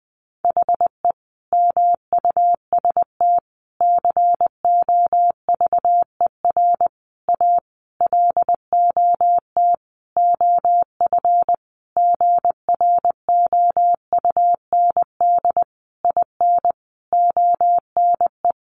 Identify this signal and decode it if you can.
Morse code, heard as a keyed tone, about 20 wpm, 700 Hz. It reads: HE MUST COVER A LOT OF GROUND IN ONE
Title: Sentences from Top 500 Words 20wpm